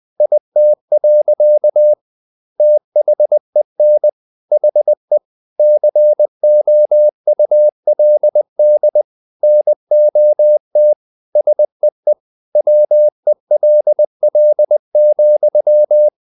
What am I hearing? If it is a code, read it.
Morse code, heard as a keyed tone, about 20 wpm, 600 Hz. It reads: IT. THEN HE COULD NOT SEE WELL,
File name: the old man and the sea part 2 20wpm